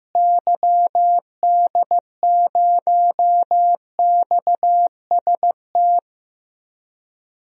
Morse code, heard as a keyed tone, about 15 wpm, 700 Hz. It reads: YD0XST
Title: DXCC-100 Call Signs - 6 Characters - Repeated 3x - 15wpm